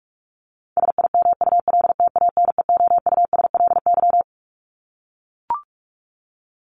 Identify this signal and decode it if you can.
Morse code, heard as a keyed tone, about 44 wpm, 700 Hz. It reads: SIMULTANEOUSLY
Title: LY Suffix - Top 100 Words with Base Words 45wpm